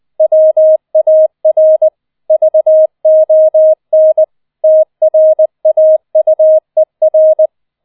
Morse code, heard as a keyed tone, about 20 wpm, 630 Hz. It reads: WAR VON TRAUER